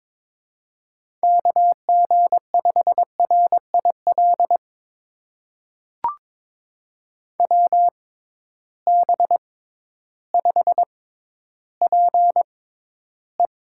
Morse code, heard as a keyed tone, about 22 wpm, spaced as for 12 wpm, 700 Hz. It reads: KG5RIL WB5PE
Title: USA Call Signs - 6 Characters - 6x Character Spacing - 22wpm